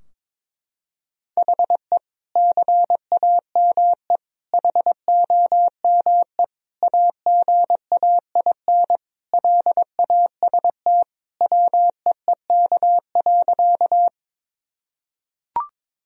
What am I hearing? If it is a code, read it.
Morse code, heard as a keyed tone, about 22 wpm, 700 Hz. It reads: HE CAME HOME AGAIN LAST WEEK.
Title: Sentences from Top 1000 Words - Review Part 2 - 22wpm